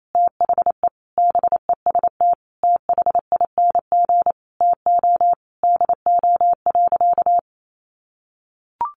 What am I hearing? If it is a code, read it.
Morse code, heard as a keyed tone, about 28 wpm, 700 Hz. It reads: THE BEST THING TO DO.
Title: Sentences from Top 1200 Words 28wpm